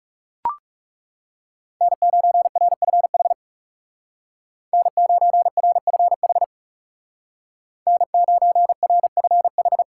Morse code, heard as a keyed tone, about 35 wpm, 700 Hz. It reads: N9RFH N9RFH N9RFH
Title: Speed Racing - USA Call Signs 30wpm